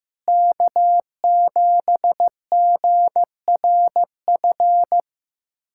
Morse code, heard as a keyed tone, about 15 wpm, 700 Hz. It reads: K7GRF
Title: USA Callsign DE Callsign - 5 Characters - 15wpm